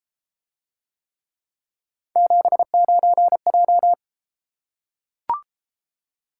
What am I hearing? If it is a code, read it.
Morse code, heard as a keyed tone, about 31 wpm, 700 Hz. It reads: 79J